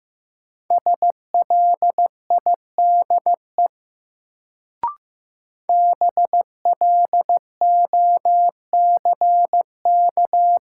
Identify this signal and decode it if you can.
Morse code, heard as a keyed tone, about 16 wpm, 700 Hz. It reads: SLIDE BLOCK
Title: Sentences from Top 1600 Words - New Word Drill Alt - 15wpm